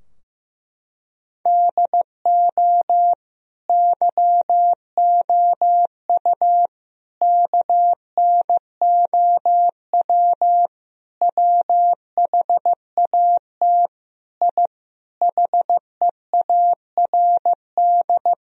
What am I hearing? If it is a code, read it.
Morse code, heard as a keyed tone, about 16 wpm, 700 Hz. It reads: DO YOU KNOW WHAT I HEARD
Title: Sentences from Top 1000 Words - Review Part 4 - 15wpm